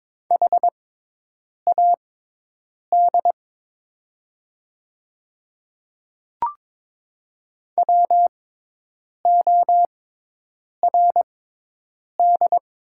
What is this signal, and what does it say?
Morse code, heard as a keyed tone, about 22 wpm, spaced as for 8 wpm, 690 Hz. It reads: HAD WORD
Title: 6x Character Spacing - Top 100 Words - 22wpm